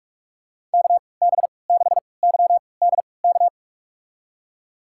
Morse code, heard as a keyed tone, about 46 wpm, spaced as for 26 wpm, 700 Hz. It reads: KB6YDK
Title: USA Call Signs - 6 Characters - 3x Character Spacing - 45wpm